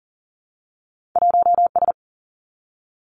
Morse code, heard as a keyed tone, about 38 wpm, 700 Hz. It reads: 1S
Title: Two Challenging Letter-Number Combinations 40wpm